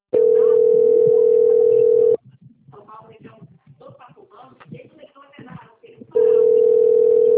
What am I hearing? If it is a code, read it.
Ringback tone (North America), 440 + 480 Hz, with speech.
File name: out- - - - - .5091